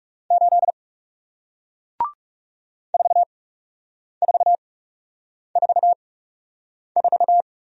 Morse code, heard as a keyed tone, about 40 wpm, spaced as for 8 wpm, 700 Hz. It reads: 8 4444